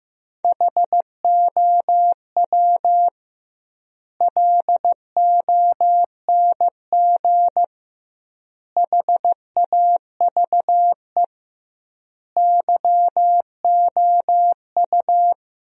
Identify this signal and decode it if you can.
Morse code, heard as a keyed tone, about 15 wpm, 700 Hz. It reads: HOW LONG HAVE YOU
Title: 2x Word-Spacing - Sentences from Top 200 Words - 15wpm